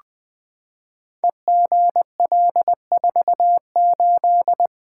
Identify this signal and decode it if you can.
Morse code, heard as a keyed tone, about 20 wpm, 700 Hz. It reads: EGL48
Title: Code Groups - 5 Letter:Numbers - 20wpm